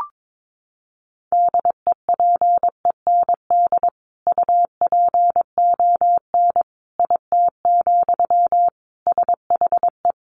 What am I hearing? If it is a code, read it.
Morse code, heard as a keyed tone, about 22 wpm, 700 Hz. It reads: DEPEND UPON IT, SHE
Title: Sentences from Top 1300 Words 22wpm